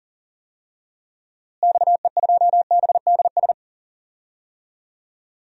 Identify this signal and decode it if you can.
Morse code, heard as a keyed tone, about 41 wpm, 700 Hz. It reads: XE2BDS
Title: DXCC-100 Call Signs - 6 Characters - Repeated 3x - 40wpm